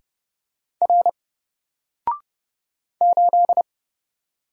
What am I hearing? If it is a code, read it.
Morse code, heard as a keyed tone, about 30 wpm, 700 Hz. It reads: R 8
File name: Single-Letter-Number-30wpm